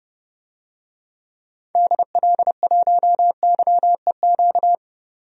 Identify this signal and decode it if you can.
Morse code, heard as a keyed tone, about 30 wpm, 700 Hz. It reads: DL1YEQ
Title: DXCC-100 Call Signs - 6 Characters - 30wpm